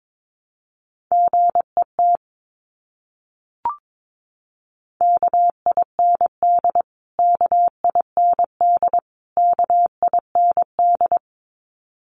Morse code, heard as a keyed tone, about 22 wpm, 700 Hz. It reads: GET KIND KIND KIND
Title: Top 500 Words Repeated 22wpm